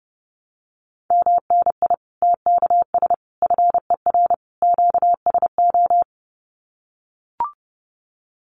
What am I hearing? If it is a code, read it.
Morse code, heard as a keyed tone, about 30 wpm, 700 Hz. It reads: MNI TKS FER QSO
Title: Common QSO Phrases 30wpm